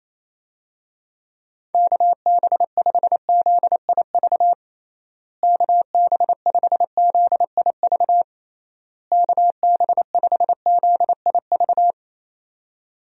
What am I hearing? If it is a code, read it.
Morse code, heard as a keyed tone, about 28 wpm, 700 Hz. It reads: KB5ZIV KB5ZIV KB5ZIV
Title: USA Call Signs - 6 Characters - Repeated 3x - 28wpm